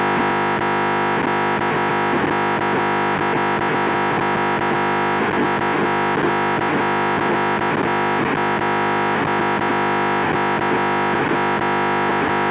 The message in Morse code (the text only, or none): none